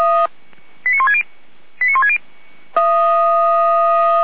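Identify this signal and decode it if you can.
Five-tone selective call (ZVEI), heard as quick runs of five tones, tones of about 70 ms. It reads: 89160 89160
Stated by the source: ZVEI five-tone selective call received on an SDR by fire brigade Feuerwehr Orlamünde.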